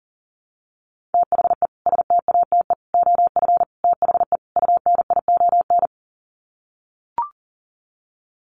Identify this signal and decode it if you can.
Morse code, heard as a keyed tone, about 37 wpm, 700 Hz. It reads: THE STATE OF THE UNION
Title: Sets of 5 Words Encore 40wpm